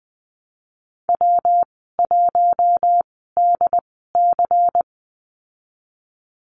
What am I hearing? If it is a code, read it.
Morse code, heard as a keyed tone, about 20 wpm, spaced as for 15 wpm, 700 Hz. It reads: W1DC
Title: 147 Lesson 24g - Letter y - Call Signs 20wpm